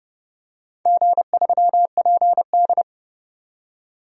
Morse code, heard as a keyed tone, about 30 wpm, 700 Hz. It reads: G3PD
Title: DXCC-100 Call Signs - 4 Characters - 30wpm